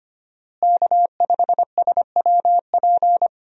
Morse code, heard as a keyed tone, about 25 wpm, 700 Hz. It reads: K5SWP
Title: USA Call Signs - 5 Characters - Repeated 2x - 25wpm